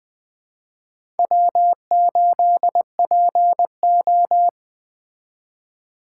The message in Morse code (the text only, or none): W8PO